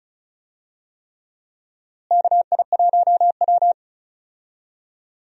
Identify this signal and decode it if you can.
Morse code, heard as a keyed tone, about 36 wpm, 700 Hz. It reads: KI1W
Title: USA Call Signs - 4 Characters - Repeated 3x - 35wpm